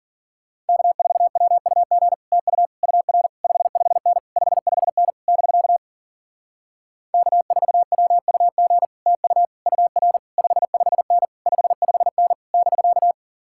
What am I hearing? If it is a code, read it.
Morse code, heard as a keyed tone, about 49 wpm, 700 Hz. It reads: K4WUG TU UR 55N 55N <BK> K4WUG TU UR 55N 55N <BK>
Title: Speed-Racing Common QSO Phrases - 40wpm